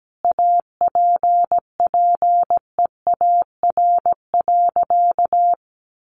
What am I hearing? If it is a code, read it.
Morse code, heard as a keyed tone, about 17 wpm, 700 Hz. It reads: APPEAR.
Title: Sentences from Top 1200 Words 17wpm